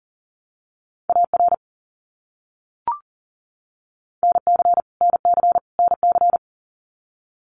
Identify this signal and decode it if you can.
Morse code, heard as a keyed tone, about 40 wpm, 700 Hz. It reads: AR NC NC NC
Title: US States Repeated 3x 40wpm